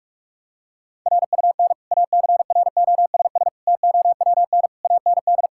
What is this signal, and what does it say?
Morse code, heard as a keyed tone, about 46 wpm, 700 Hz. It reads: RUN ACROSS TOWN AND
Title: Sentences from Top 1000 Words - Review Part 5 - 45wpm